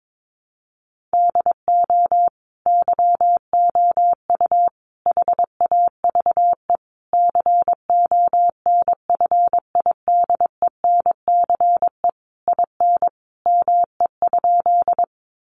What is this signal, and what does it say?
Morse code, heard as a keyed tone, about 22 wpm, 700 Hz. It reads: DO YOU HAVE CONFIDENCE IN ME?
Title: Sentences from Top 1500 Words 22wpm